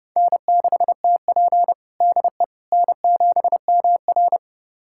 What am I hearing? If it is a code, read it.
Morse code, heard as a keyed tone, about 30 wpm, 700 Hz. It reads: N6TP DE N7MR
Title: USA Callsign DE Callsign - 4 Characters - 30wpm